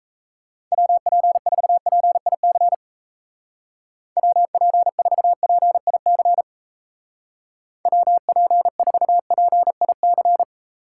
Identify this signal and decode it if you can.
Morse code, heard as a keyed tone, about 35 wpm, 700 Hz. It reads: WP4PIC WP4PIC WP4PIC
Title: Speed Racing - USA 2x3 Call Signs 28wpm